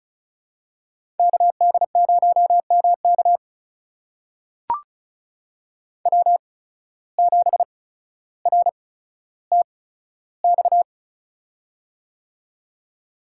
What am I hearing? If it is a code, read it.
Morse code, heard as a keyed tone, about 36 wpm, spaced as for 15 wpm, 690 Hz. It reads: KD0MK W7RTX
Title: USA Call Signs - 5 Characters - 8x Character Spacing - 35wpm